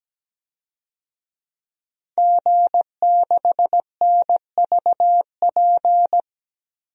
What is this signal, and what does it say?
Morse code, heard as a keyed tone, about 17 wpm, 700 Hz. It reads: G6NVP
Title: DXCC-100 Call Signs - 5 Characters - 17wpm